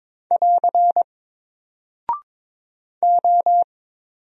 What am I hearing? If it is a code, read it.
Morse code, heard as a keyed tone, about 22 wpm, 700 Hz. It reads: <AR> O
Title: Single Character (Letter:Number:Punctuation:Prosign) Alt 22wpm